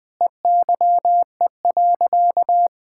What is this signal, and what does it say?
Morse code, heard as a keyed tone, about 20 wpm, 700 Hz. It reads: EYE.
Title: Sentences with 5-Word Phrases - Spoken Beginning - 20wpm